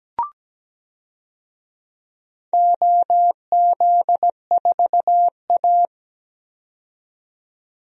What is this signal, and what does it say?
Morse code, heard as a keyed tone, about 17 wpm, 700 Hz. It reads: OZ4A
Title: DXCC-100 Call Signs - 4 Characters - 17wpm